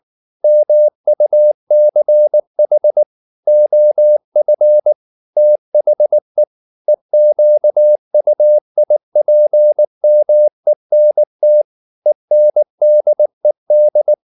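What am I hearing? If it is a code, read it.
Morse code, heard as a keyed tone, about 19 wpm, 600 Hz. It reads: MUCH OF THE EQUIPMENT ENDED
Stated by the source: Morse code at 19 wpm from CW med Gnister, reading Ken Follett's novel Never in English.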